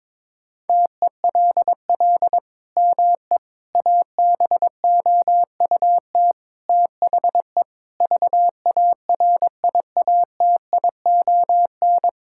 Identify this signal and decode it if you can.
Morse code, heard as a keyed tone, about 22 wpm, 700 Hz. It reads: TELL ME ABOUT THE VARIATION
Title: Sentences from Top 1600 Words 22wpm